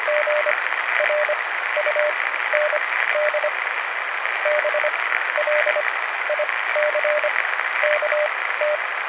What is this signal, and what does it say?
Morse code, heard as a keyed tone, about 25 wpm, spaced as for 14 wpm, 600 Hz, with radio noise. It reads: GRUND BLICKT